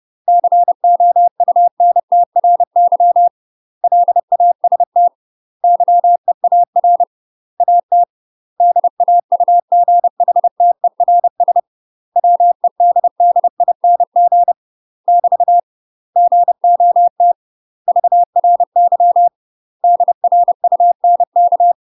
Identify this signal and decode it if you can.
Morse code, heard as a keyed tone, about 30 wpm, 700 Hz. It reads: COUNTRY LAST YEAR AT DAUGHTERS WEDDING <BT> GOT VRY DRUNK
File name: UK MASTER QSOs TRAINING / 30WPM / UK MASTER QSOs 0014